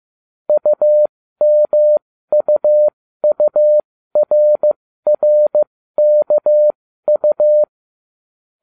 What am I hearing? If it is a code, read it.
Morse code, heard as a keyed tone, about 15 wpm, spaced as for 13 wpm, 600 Hz. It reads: UMUURRKU